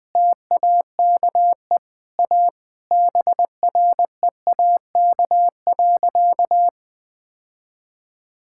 Morse code, heard as a keyed tone, about 20 wpm, 700 Hz. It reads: TAKE A BREAK.
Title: Sentences with 4-Word Phrases - 20wpm